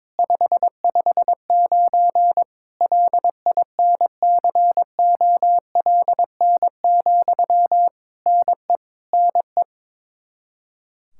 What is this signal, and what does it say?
Morse code, heard as a keyed tone, about 22 wpm, 700 Hz. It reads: 559 LINCOLN, NE NE
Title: Practice QSO - Exchange 1 - Beginning - 22wpm